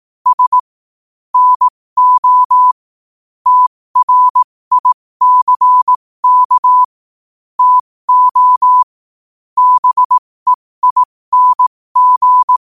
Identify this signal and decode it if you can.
Morse code, heard as a keyed tone, about 18 wpm, spaced as for 15 wpm, 1000 Hz. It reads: S NO TRICK TO BEING
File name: qotdfiles / 15wpm